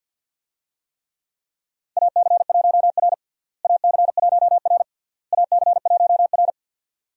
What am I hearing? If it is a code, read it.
Morse code, heard as a keyed tone, about 50 wpm, 700 Hz. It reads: AC1R AC1R AC1R
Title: USA Call Signs - 4 Characters - Repeated 3x - 50wpm